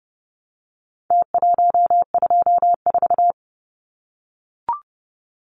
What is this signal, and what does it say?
Morse code, heard as a keyed tone, about 30 wpm, 700 Hz. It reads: T124
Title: Numbers - Contest Style - 4-Digit 30wpm